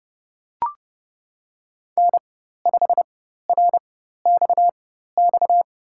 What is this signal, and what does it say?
Morse code, heard as a keyed tone, about 31 wpm, spaced as for 14 wpm, 700 Hz. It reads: N5RXX
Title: USA Call Signs - 5 Characters - 4x Character Spacing - 30wpm